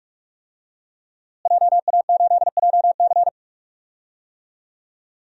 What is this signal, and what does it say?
Morse code, heard as a keyed tone, about 46 wpm, 700 Hz. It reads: JA8JC
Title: DXCC-100 Call Signs - 5 Characters - Repeated 2x - 45wpm